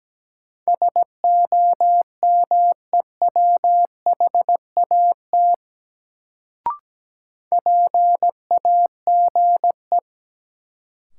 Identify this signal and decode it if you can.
Morse code, heard as a keyed tone, about 17 wpm, 700 Hz. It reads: SOMEWHAT PAGE
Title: Sentences from Top 1000 Words - Review Part 1 - New Word Drill - 17wpm